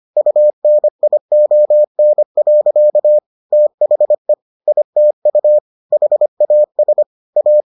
Morse code, heard as a keyed tone, about 25 wpm, 600 Hz. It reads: UNION. THE ITU HAS A